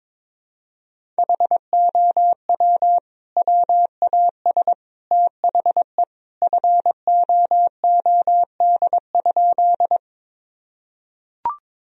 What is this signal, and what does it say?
Morse code, heard as a keyed tone, about 22 wpm, 700 Hz. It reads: HOW WAS THE FOOD?
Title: Sentences from Top 1000 Words - Review Part 1 - 22wpm